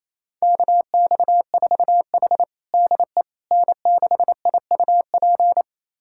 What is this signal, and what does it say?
Morse code, heard as a keyed tone, about 27 wpm, 700 Hz. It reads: KX4H DE N6IUP
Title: USA Callsign DE Callsign 28wpm